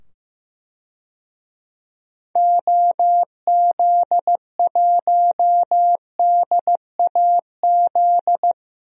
Morse code, heard as a keyed tone, about 15 wpm, 700 Hz. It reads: OZ1DAZ
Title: DXCC-100 Call Signs - 6 Characters - 15wpm